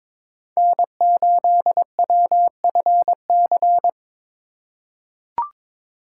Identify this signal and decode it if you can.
Morse code, heard as a keyed tone, about 21 wpm, 700 Hz. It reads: N8WFC
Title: USA Call Signs - 5 Characters - 3x Character Spacing - 22wpm